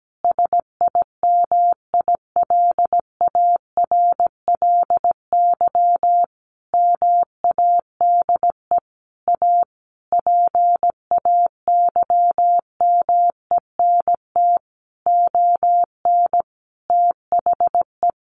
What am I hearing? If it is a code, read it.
Morse code, heard as a keyed tone, about 17 wpm, 700 Hz. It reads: SIMILARLY MADE A PAYMENT ON THE
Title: Sentences from Top 1400 Words 17wpm